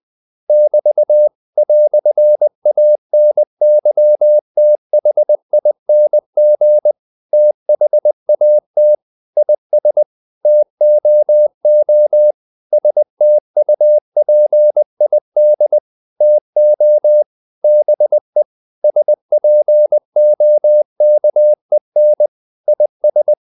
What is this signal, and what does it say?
Morse code, heard as a keyed tone, about 20 wpm, 600 Hz. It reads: <BT> "ANYTHING THAT IS TOO STUPID TO BE SPOKEN IS
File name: CW / Practice / EnglishQuotations0320